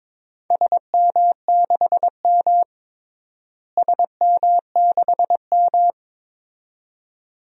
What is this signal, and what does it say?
Morse code, heard as a keyed tone, about 22 wpm, 700 Hz. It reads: SM6M SM6M
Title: DXCC-100 Call Signs - 4 Characters - Repeated 2x - 22wpm